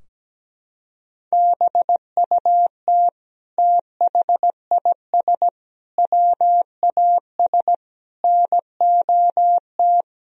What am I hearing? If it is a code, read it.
Morse code, heard as a keyed tone, about 17 wpm, 700 Hz. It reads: BUT THIS WAS NOT
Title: Sentences from Top 1000 Words - Review Part 4 - 17wpm